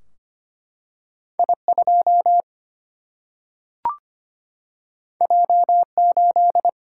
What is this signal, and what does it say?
Morse code, heard as a keyed tone, about 25 wpm, 700 Hz. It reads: I2 J8
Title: Single-Letter Single-Number 25wpm